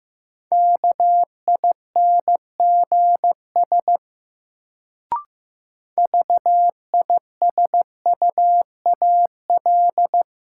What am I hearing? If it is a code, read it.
Morse code, heard as a keyed tone, about 15 wpm, 700 Hz. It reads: KINGS VISUAL
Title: Sentences from Top 1000 Words - Review Part 6 - New Word Drill - 15wpm